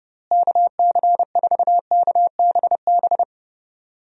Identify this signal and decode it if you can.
Morse code, heard as a keyed tone, about 32 wpm, 700 Hz. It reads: KC4KBB
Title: USA Call Signs - 6 Characters - Repeated 3x - 30wpm